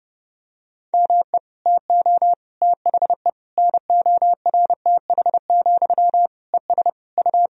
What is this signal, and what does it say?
Morse code, heard as a keyed tone, about 30 wpm, 700 Hz. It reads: ME TO THE NORTH, ES U
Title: Sentences from Top 400 Words and Abbreviations 30wpm